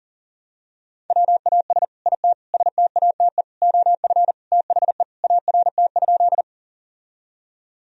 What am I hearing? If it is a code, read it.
Morse code, heard as a keyed tone, about 40 wpm, 700 Hz. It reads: WAS IT STATE OF THE ART?
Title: Sentences from Top 1000 Words - Review Part 3 - 40wpm